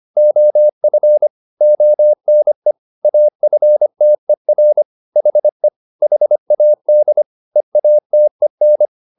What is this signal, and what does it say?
Morse code, heard as a keyed tone, about 25 wpm, 600 Hz. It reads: OF ONE AFTER HE HAD EATEN